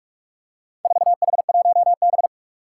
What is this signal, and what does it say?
Morse code, heard as a keyed tone, about 47 wpm, 700 Hz. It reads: 4H1B
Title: Four Challenging Letter-Number Combinations 45wpm